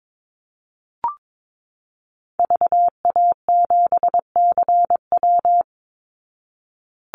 Morse code, heard as a keyed tone, about 22 wpm, 700 Hz. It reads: VA7CW
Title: Call Signs Repeated 3x - 22wpm